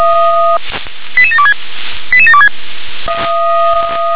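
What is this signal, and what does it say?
Five-tone selective call (ZVEI), heard as quick runs of five tones, tones of about 70 ms. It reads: 88726 88726
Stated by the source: Received on SDR as a ZVEI five-tone selective call, address 88126.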